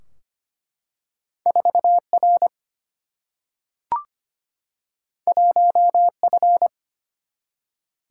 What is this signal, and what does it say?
Morse code, heard as a keyed tone, about 25 wpm, 700 Hz. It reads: 4R 1F
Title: Single-Letter Single-Number 25wpm